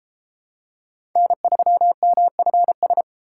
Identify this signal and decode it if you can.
Morse code, heard as a keyed tone, about 30 wpm, 700 Hz. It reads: N3MFS